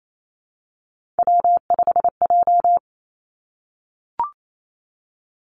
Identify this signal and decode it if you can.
Morse code, heard as a keyed tone, about 28 wpm, 700 Hz. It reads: W5J
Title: Three Challenging Letter-Number Combinations 28wpm